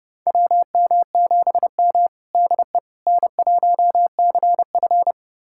Morse code, heard as a keyed tone, about 32 wpm, 700 Hz. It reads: WM7M DE N1CF
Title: USA Callsign DE Callsign - 4 Characters - 30wpm